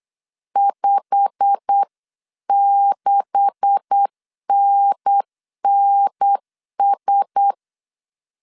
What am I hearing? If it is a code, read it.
Morse code, heard as a keyed tone, about 8 wpm, 800 Hz. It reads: 56NNS
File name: CW / DET / 1 9 1500C S C 8wpm 5653